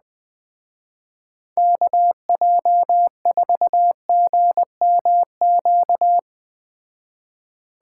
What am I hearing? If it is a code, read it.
Morse code, heard as a keyed tone, about 19 wpm, 700 Hz. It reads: KJ4GMQ
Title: Speed Racing - USA 2x3 Call Signs 17wpm